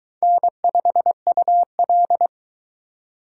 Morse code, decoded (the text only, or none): N5UL